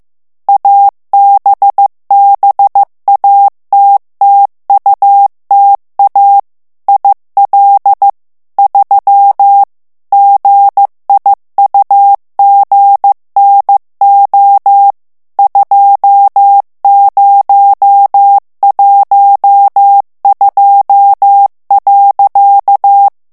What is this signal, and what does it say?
Morse code, heard as a keyed tone, about 15 wpm, 800 Hz. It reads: ABBATTUTA IL 3 GIUGNO 2012.
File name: morse mp3 / ITALIEN texte clair 1 vitesse 14 mots minute